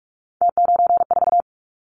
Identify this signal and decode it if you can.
Morse code, heard as a keyed tone, about 46 wpm, 700 Hz. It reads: T94
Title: Callsign Prefixes for Top 50 DXCC - 45wpm